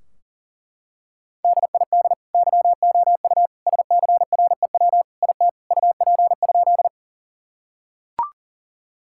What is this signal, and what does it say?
Morse code, heard as a keyed tone, about 40 wpm, 700 Hz. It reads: DID YOU SCREW IT UP?
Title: Sentences from Top 1600 Words 40wpm